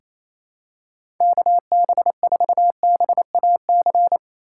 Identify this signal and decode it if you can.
Morse code, heard as a keyed tone, about 28 wpm, 700 Hz. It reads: KB4BAC
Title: USA Call Signs - 6 Characters - 3x Character Spacing - 28wpm